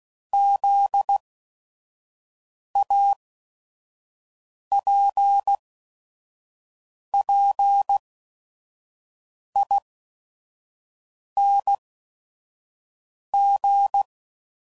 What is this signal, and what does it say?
Morse code, heard as a keyed tone, about 16 wpm, spaced as for 5 wpm, 800 Hz. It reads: ZAPPING
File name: Suffixes / Special ING's / VerySlowToFaster ING's